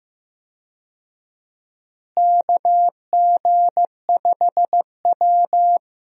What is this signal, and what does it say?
Morse code, heard as a keyed tone, about 15 wpm, 700 Hz. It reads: KG5W
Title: USA Call Signs - 4 Characters - Repeated 2x - 15wpm